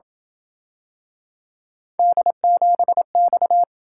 Morse code, heard as a keyed tone, about 27 wpm, 700 Hz. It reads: D7X